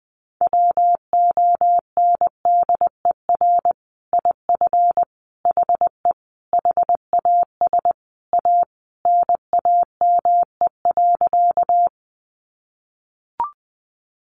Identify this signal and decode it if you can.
Morse code, heard as a keyed tone, about 20 wpm, 700 Hz. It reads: WONDER IF HE HAS A NAME.
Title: Sentences from Top 500 Words 20wpm